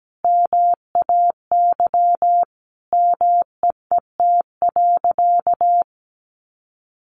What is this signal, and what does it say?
Morse code, heard as a keyed tone, about 17 wpm, 700 Hz. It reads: MAY MEET.
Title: Sentences from Top 1100 Words 17wpm